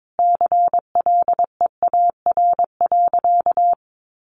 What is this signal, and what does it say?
Morse code, heard as a keyed tone, about 22 wpm, 700 Hz. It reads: CLEAR.
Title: Sentences from Top 1200 Words 22wpm